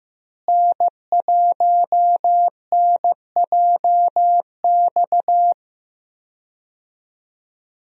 Morse code, heard as a keyed tone, about 15 wpm, 700 Hz. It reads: N1NJX
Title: USA Call Signs - 5 Characters - Repeated 2x - 15wpm